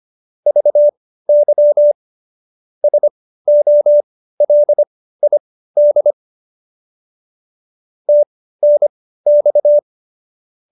Morse code, heard as a keyed tone, about 25 wpm, spaced as for 15 wpm, 590 Hz. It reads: VY SOLID TNX